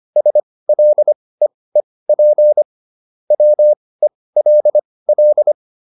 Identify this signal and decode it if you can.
Morse code, heard as a keyed tone, about 25 wpm, spaced as for 18 wpm, 600 Hz. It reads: SLEEP WELL